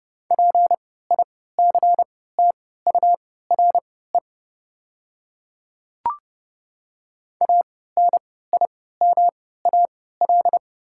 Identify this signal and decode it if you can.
Morse code, heard as a keyed tone, about 29 wpm, spaced as for 17 wpm, 690 Hz. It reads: PICTURE ANIMAL
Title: 3x Character Spacing - Top 200 Words - 30wpm